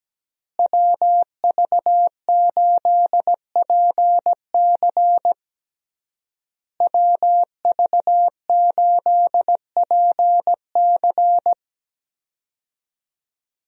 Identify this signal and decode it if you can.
Morse code, heard as a keyed tone, about 17 wpm, 700 Hz. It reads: WV8PC WV8PC
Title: USA Call Signs - 5 Characters - Repeated 3x - 17wpm